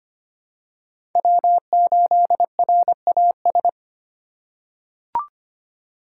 Morse code, heard as a keyed tone, about 25 wpm, 700 Hz. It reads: W8RAS
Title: USA Call Signs - 4x Character Spacing - 25wpm